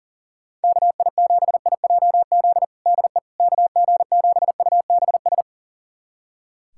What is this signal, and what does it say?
Morse code, heard as a keyed tone, about 37 wpm, 700 Hz. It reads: KI7IJZ DE KG7UBS
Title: USA Callsign DE Callsign 40wpm